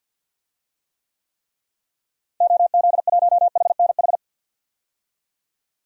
Morse code, heard as a keyed tone, about 48 wpm, 700 Hz. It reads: OZ1HNH
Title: DXCC-100 Call Signs - 6 Characters - Repeated 3x - 50wpm